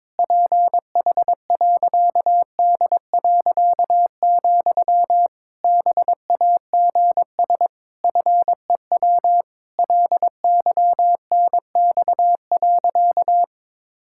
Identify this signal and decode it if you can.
Morse code, heard as a keyed tone, about 22 wpm, 700 Hz. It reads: PH.D., BAGS FEW LYNX.
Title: Pangram Challenge v2 - 22wpm